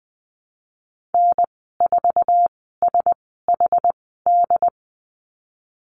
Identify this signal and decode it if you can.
Morse code, heard as a keyed tone, about 20 wpm, spaced as for 15 wpm, 700 Hz. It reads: N4SHD
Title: 076 Lesson 13g - Letter L - Call Signs 20wpm